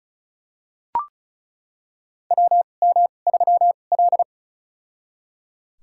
Morse code, heard as a keyed tone, about 34 wpm, spaced as for 25 wpm, 700 Hz. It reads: WM3L
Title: USA Call Signs - 4 Characters - 2x Character Spacing - 35wpm